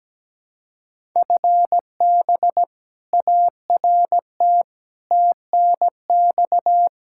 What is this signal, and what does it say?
Morse code, heard as a keyed tone, about 17 wpm, 700 Hz. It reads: FB ART TNX
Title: K1USN SST QSO Practice 17wpm